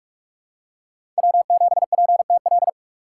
Morse code, heard as a keyed tone, about 46 wpm, 700 Hz. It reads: W7PTL
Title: USA Call Signs - 5 Characters - 6x Character Spacing - 45wpm